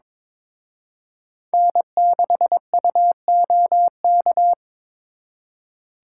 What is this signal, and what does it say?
Morse code, heard as a keyed tone, about 26 wpm, 700 Hz. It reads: N6UOK
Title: Speed Racing - USA 1x3 Call Signs 22wpm